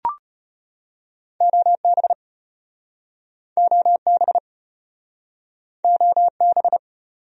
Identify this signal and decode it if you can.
Morse code, heard as a keyed tone, about 36 wpm, 700 Hz. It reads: OB OB OB